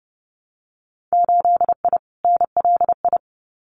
Morse code, heard as a keyed tone, about 30 wpm, 700 Hz. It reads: 8I NLI
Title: Winter Field Day Exchanges Encore 30wpm